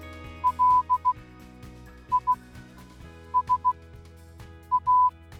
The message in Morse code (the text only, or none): LISA